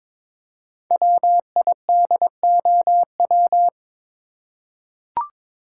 Morse code, heard as a keyed tone, about 22 wpm, 690 Hz. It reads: WIDOW
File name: 5 Letter Words - First 3 Letters Given - 22wpm